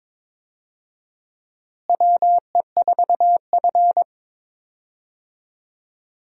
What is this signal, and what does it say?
Morse code, heard as a keyed tone, about 22 wpm, 700 Hz. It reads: WE4F
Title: USA Call Signs - 4 Characters - Repeated 3x - 22wpm